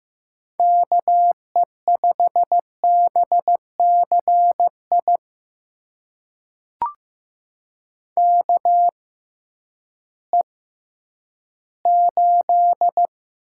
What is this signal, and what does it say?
Morse code, heard as a keyed tone, about 15 wpm, spaced as for 10 wpm, 700 Hz. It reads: KE5BCI KE8
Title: USA Call Signs - 6 Characters - 6x Character Spacing - 15wpm